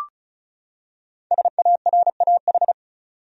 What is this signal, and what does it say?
Morse code, heard as a keyed tone, about 34 wpm, 700 Hz. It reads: SARAH
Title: USA Common Names - 35wpm